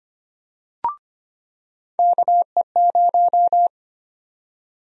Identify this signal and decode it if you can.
Morse code, heard as a keyed tone, about 25 wpm, 700 Hz. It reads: KE0
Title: USA Call Sign Prefixes - 25wpm